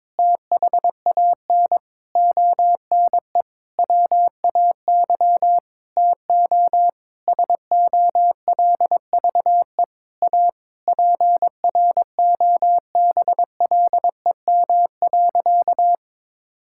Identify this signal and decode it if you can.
Morse code, heard as a keyed tone, about 22 wpm, 700 Hz. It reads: THAN ONE WAY TO SOLVE A PROBLEM.
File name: Sentences with 9-Word Phrases - 22wpm